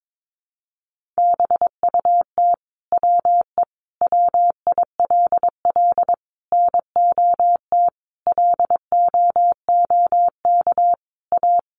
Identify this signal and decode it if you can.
Morse code, heard as a keyed tone, about 22 wpm, 700 Hz. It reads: BUT WE WILL NOT LOOK A
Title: Sentences from Top 1500 Words 22wpm